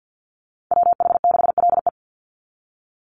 Morse code, heard as a keyed tone, about 50 wpm, 700 Hz. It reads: WH6LE
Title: NA SOTA Activator Callsigns 50wpm